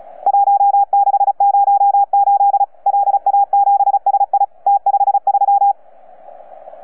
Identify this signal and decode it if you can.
Morse code, heard as a keyed tone, about 36 wpm, 800 Hz. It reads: 1608 LA7SI T53
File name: UP2L-20-CW